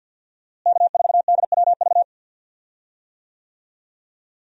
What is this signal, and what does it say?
Morse code, heard as a keyed tone, about 50 wpm, 700 Hz. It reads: K4DWV
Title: USA Call Signs - 5 Characters - Repeated 2x - 50wpm